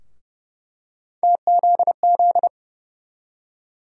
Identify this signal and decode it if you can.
Morse code, heard as a keyed tone, about 30 wpm, 700 Hz. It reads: TZZ
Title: USA Call Sign Postfixes - 3 Letters 30wpm